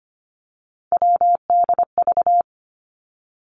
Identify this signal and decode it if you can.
Morse code, heard as a keyed tone, about 25 wpm, 700 Hz. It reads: WDV
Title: Three Challenging Letter-Number Combinations 25wpm